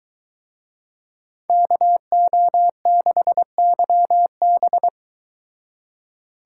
Morse code, heard as a keyed tone, about 23 wpm, 700 Hz. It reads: KO6YB